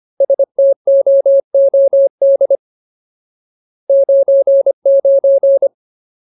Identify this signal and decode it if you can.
Morse code, heard as a keyed tone, about 25 wpm, 550 Hz. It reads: STOOD 99
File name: Ultimate Word List / 25WPM Ultimate